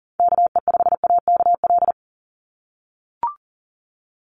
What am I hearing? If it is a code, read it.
Morse code, heard as a keyed tone, about 40 wpm, 700 Hz. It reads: KE5AKL
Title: NA SOTA Activator Callsigns 40wpm